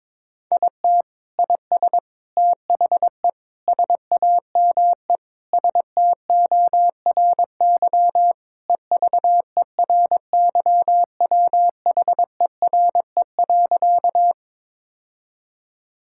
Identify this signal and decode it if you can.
Morse code, heard as a keyed tone, about 22 wpm, 700 Hz. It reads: IT IS THE SAME STORY EVERYWHERE.
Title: Sentences from Top 1600 Words 22wpm